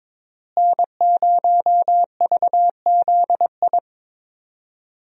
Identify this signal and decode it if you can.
Morse code, heard as a keyed tone, about 22 wpm, 700 Hz. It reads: N0VZI
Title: USA Call Signs - 5 Characters - Repeated 2x - 22wpm